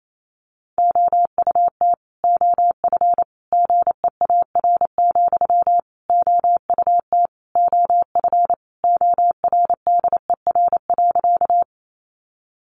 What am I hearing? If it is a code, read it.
Morse code, heard as a keyed tone, about 28 wpm, 700 Hz. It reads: OUT OF GEAR, OUT OF ORDER.
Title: Sentences from Top 1400 Words 28wpm